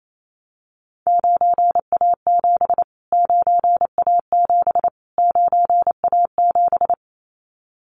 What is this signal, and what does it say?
Morse code, heard as a keyed tone, about 28 wpm, 700 Hz. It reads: 9A7 9A7 9A7
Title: Callsign Prefixes for Top 25 DXCC - Repeated 3x - 28wpm